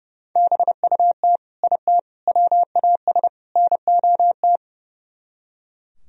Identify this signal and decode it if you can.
Morse code, heard as a keyed tone, about 38 wpm, 700 Hz. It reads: BUT IT WAS NOT